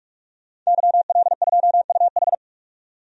Morse code, heard as a keyed tone, about 46 wpm, 700 Hz. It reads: YL2UH
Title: DXCC-100 Call Signs - 5 Characters - 45wpm